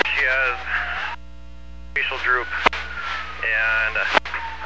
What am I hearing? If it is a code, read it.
Emergency Alert System (SAME) tones, heard as harsh buzzing bursts, with speech. It is heard from 0.6 to 4.1 s.